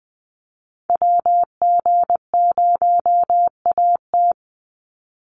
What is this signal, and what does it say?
Morse code, heard as a keyed tone, about 20 wpm, 700 Hz. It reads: WG0AT
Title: NA SOTA Activator Callsigns 20wpm